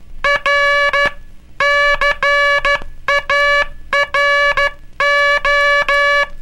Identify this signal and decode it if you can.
Morse code, heard as a keyed tone, about 12 wpm, 1700 Hz, with noise. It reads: R CARO
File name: cursos / Curso CW mp3 / CW Aula - 18